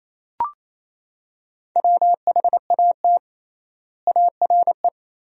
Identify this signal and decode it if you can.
Morse code, heard as a keyed tone, about 28 wpm, 700 Hz. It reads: WHAT ARE
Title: 3x Word-Spacing - Sentences from Top 100 Words - 28wpm